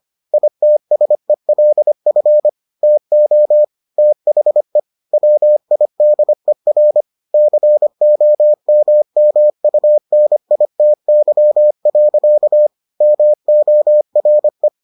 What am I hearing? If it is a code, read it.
Morse code, heard as a keyed tone, about 25 wpm, 600 Hz. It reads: ITSELF TO THE WIDER COMMUNITY. MORE